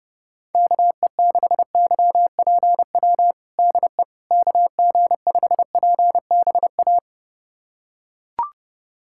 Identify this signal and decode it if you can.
Morse code, heard as a keyed tone, about 28 wpm, 700 Hz. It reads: KE6YPW DE KG5PBA
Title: USA Callsign DE Callsign - 6 Characters - 30wpm